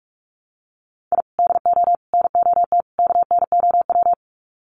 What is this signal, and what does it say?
Morse code, heard as a keyed tone, about 44 wpm, 700 Hz. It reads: I DO NOT KNOW
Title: Sets of 4 Words 45wpm